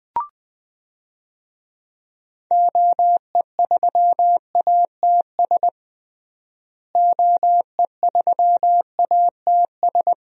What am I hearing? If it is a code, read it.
Morse code, heard as a keyed tone, about 20 wpm, 700 Hz. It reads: OE3ATS OE3ATS
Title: DXCC-100 Call Signs - 6 Characters - Repeated 3x - 20wpm